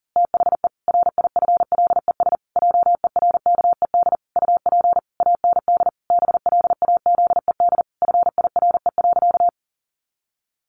Morse code, heard as a keyed tone, about 40 wpm, 700 Hz. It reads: THE RIFLES JERKED UP AND BLAZED FIRE.
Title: A Sound Of Thunder 40wpm